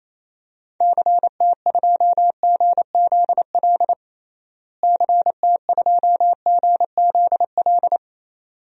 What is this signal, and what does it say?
Morse code, heard as a keyed tone, about 28 wpm, 700 Hz. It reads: CT2GZL CT2GZL
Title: DXCC-100 Call Signs - 6 Characters - Repeated 2x - 28wpm